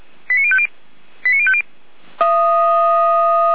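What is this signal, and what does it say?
Five-tone selective call (ZVEI), heard as quick runs of five tones, tones of about 70 ms. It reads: 89050 89050
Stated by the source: ZVEI five-tone call received on SDR.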